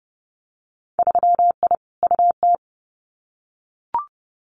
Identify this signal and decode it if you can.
Morse code, heard as a keyed tone, about 30 wpm, 700 Hz. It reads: 3I UT
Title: Winter Field Day Exchanges Encore 30wpm